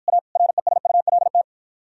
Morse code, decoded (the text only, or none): A RESULT